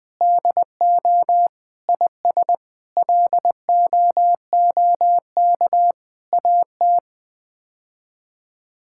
Morse code, heard as a keyed tone, about 20 wpm, 700 Hz. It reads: DO IS LOOK AT
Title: Sets of 8 Words - 20wpm